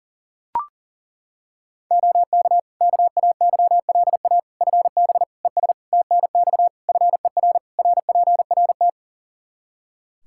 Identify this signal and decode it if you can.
Morse code, heard as a keyed tone, about 40 wpm, 700 Hz. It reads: OK KAYLA FB ES TNX FER RPRT
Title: K1USN SST QSO Practice 40wpm